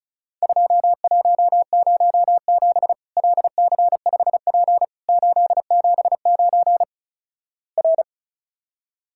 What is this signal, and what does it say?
Morse code, heard as a keyed tone, about 34 wpm, 700 Hz. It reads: 2107 LC5P 879 R
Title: Worked All Europe DX Contest - QTC Exchange - 35wpm